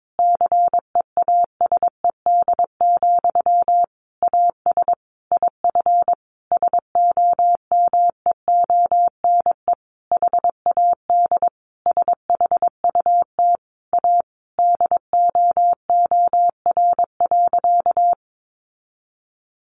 Morse code, heard as a keyed tone, about 22 wpm, 700 Hz. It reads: CEASED, AS IF SOMEONE HAD SHUT A DOOR.
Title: A Sound Of Thunder 22wpm